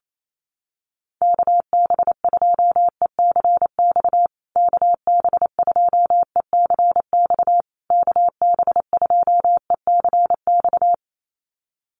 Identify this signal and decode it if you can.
Morse code, heard as a keyed tone, about 28 wpm, 700 Hz. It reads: KB2ECX KB2ECX KB2ECX
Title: USA Call Signs Repeated 3x 28wpm